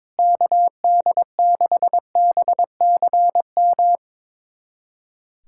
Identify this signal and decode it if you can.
Morse code, heard as a keyed tone, about 22 wpm, 700 Hz. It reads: KD6BCM
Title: USA Callsign DE Callsign - 6 Characters - 22wpm